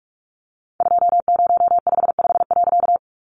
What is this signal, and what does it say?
Morse code, heard as a keyed tone, about 44 wpm, 700 Hz. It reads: 2055.
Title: A Sound Of Thunder 45wpm